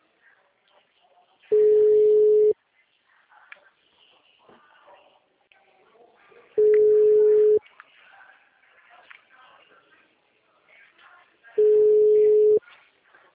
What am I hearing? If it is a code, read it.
Ringback tone (Europe, 425 Hz), with speech.